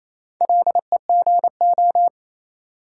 Morse code, heard as a keyed tone, about 27 wpm, 700 Hz. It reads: LEGO
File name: Easy Words - Speed-Racing with Hints - 22wpm